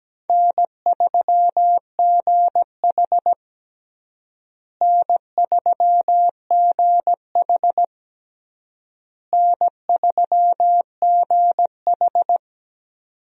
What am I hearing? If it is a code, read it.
Morse code, heard as a keyed tone, about 17 wpm, 700 Hz. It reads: N3GH N3GH N3GH
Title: USA Call Signs - 4 Characters - Repeated 3x - 17wpm